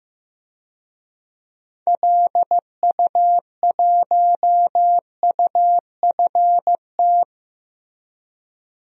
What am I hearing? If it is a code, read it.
Morse code, heard as a keyed tone, about 15 wpm, 700 Hz. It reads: LU1UFT
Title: DXCC-100 Call Signs - 6 Characters - Repeated 2x - 15wpm